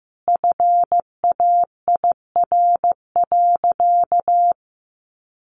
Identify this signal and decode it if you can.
Morse code, heard as a keyed tone, about 15 wpm, 700 Hz. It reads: FAIR.
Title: Sentences from Top 1300 Words 15wpm